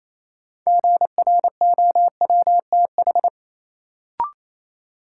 Morse code, heard as a keyed tone, about 28 wpm, 700 Hz. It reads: GROWTH
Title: Sentences from Top 1300 Words - New Word Drill Alt - 28wpm